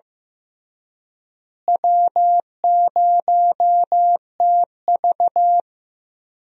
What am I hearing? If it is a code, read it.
Morse code, heard as a keyed tone, about 15 wpm, 700 Hz. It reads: W0TV
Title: USA Call Signs - 4 Characters - Repeated 3x - 15wpm